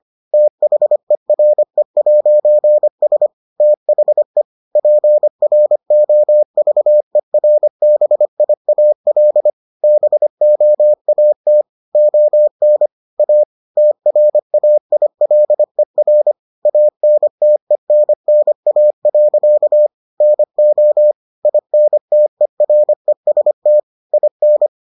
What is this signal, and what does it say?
Morse code, heard as a keyed tone, about 25 wpm, 600 Hz. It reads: THERE'S THE PROVERBIAL BOAT ON A TRAILER ANTENNA. NO INTEREST IN